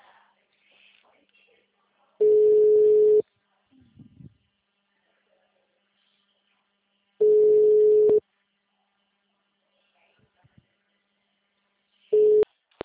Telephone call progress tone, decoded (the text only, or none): ringback tone